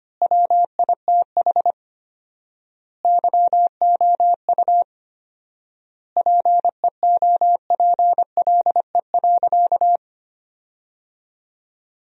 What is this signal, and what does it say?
Morse code, heard as a keyed tone, about 25 wpm, 700 Hz. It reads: WITH YOU PEOPLE.
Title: 4x Word-Spacing - Sentences from Top 300 Words - 25wpm